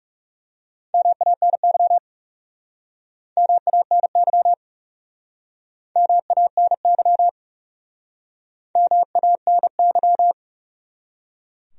Morse code, heard as a keyed tone, about 40 wpm, 700 Hz. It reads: MANY MANY MANY MANY